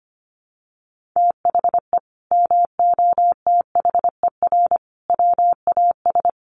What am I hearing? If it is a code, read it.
Morse code, heard as a keyed tone, about 25 wpm, 700 Hz. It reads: THE MOTHER WAS
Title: Sentences from Top 1500 Words 25wpm